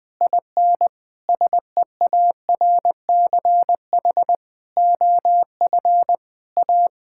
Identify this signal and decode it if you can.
Morse code, heard as a keyed tone, about 21 wpm, 700 Hz. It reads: IN SEARCH OF A
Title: Sentences with 8-Word Phrases - Spoken Beginning - 20wpm